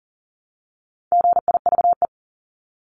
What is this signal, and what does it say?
Morse code, heard as a keyed tone, about 40 wpm, 700 Hz. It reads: GIVE
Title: ES:S Suffix - Top 100 Words with Base Words 40wpm